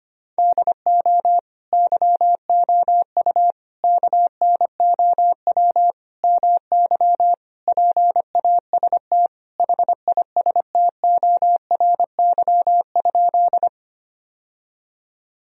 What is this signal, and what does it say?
Morse code, heard as a keyed tone, about 25 wpm, 700 Hz. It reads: DO YOU KNOW MY PAST HISTORY?
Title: Sentences from Top 1000 Words - Review Part 3 - 25wpm